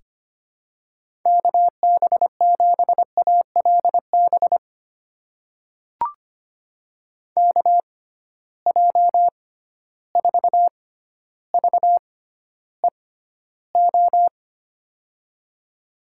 Morse code, heard as a keyed tone, about 25 wpm, spaced as for 13 wpm, 700 Hz. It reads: KB7ALB KJ4VEO
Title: USA Call Signs - 6 Characters - 6x Character Spacing - 25wpm